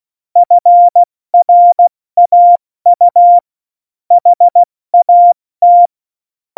Morse code, heard as a keyed tone, about 16 wpm, 700 Hz. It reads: FRAU HAT
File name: Morsekurs / Witze / Witz 16. Januar 2018, 70 F80